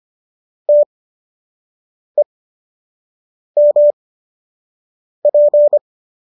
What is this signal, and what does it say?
Morse code, heard as a keyed tone, about 25 wpm, spaced as for 6 wpm, 590 Hz. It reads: TEMP
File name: qso003 06